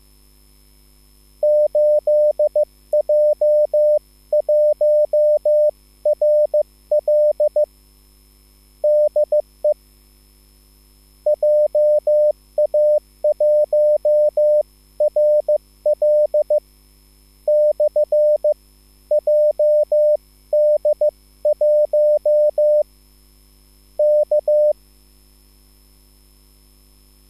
Morse code, heard as a keyed tone, about 15 wpm, spaced as for 12 wpm, 600 Hz. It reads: 8J1RL DE JA1RL/JD1 K